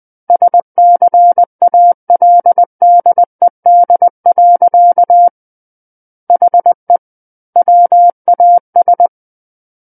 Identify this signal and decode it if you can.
Morse code, heard as a keyed tone, about 20 wpm, 700 Hz. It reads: SCALDED. HE WAS
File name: 20 WPM